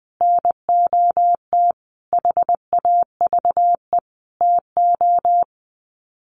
Morse code, heard as a keyed tone, about 20 wpm, 700 Hz. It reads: NOT HAVE TO
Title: Sets of 5 Words 20wpm